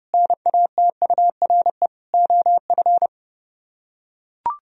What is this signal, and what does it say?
Morse code, heard as a keyed tone, about 29 wpm, 700 Hz. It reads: NATURE OF